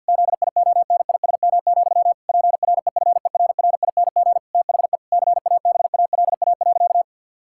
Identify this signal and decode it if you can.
Morse code, heard as a keyed tone, about 50 wpm, 700 Hz. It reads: ZIONISM, PREFERRING THE CABALA.
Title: Pangram Challenge v2 - 50wpm